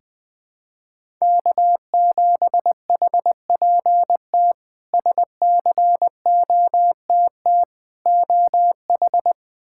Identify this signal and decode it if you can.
Morse code, heard as a keyed tone, about 20 wpm, 700 Hz. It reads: K7HPT SCOTT OH
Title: K1USN SST QSO Practice - 20wpm